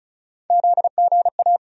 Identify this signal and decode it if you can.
Morse code, heard as a keyed tone, about 36 wpm, 700 Hz. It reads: ZGA
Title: USA Call Sign Postfixes - 3 Letters 35wpm